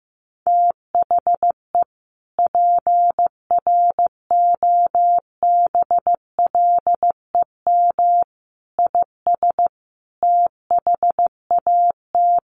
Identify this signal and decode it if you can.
Morse code, heard as a keyed tone, about 15 wpm, 700 Hz. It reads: THE PROBLEM IS THAT
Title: Sets of 5 Words Encore2 15wpm